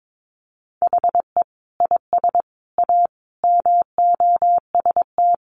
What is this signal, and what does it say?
Morse code, heard as a keyed tone, about 22 wpm, 700 Hz. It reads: HE IS A MOST
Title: Sentences from Top 1000 Words 22wpm